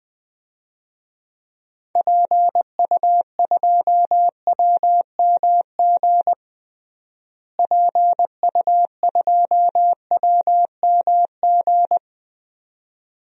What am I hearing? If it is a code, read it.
Morse code, heard as a keyed tone, about 20 wpm, 700 Hz. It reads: PU2WMG PU2WMG
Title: DXCC-100 Call Signs - 6 Characters - Repeated 2x - 20wpm